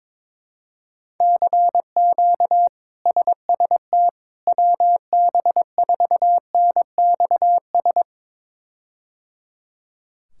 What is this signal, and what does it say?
Morse code, heard as a keyed tone, about 22 wpm, 700 Hz. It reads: CQ SST WB4NXS
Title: K1USN SST QSO Practice - 22wpm